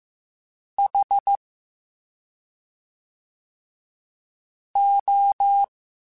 Morse code, heard as a keyed tone, about 15 wpm, 800 Hz. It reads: H O